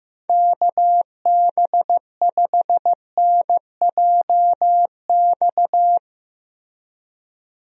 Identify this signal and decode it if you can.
Morse code, heard as a keyed tone, about 15 wpm, 700 Hz. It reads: KB5NJX
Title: USA Call Signs - 6 Characters - Repeated 2x - 15wpm